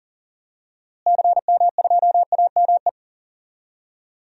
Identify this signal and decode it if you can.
Morse code, heard as a keyed tone, about 40 wpm, 700 Hz. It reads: CM2AME
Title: DXCC-100 Call Signs - 6 Characters - Repeated 2x - 40wpm